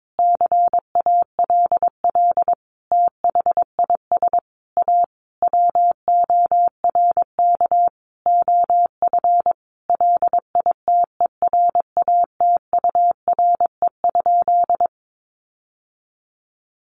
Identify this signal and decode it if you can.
Morse code, heard as a keyed tone, about 22 wpm, 700 Hz. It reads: CALL THIS A WORK OF LITERATURE?
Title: Sentences from Top 600 Words 22wpm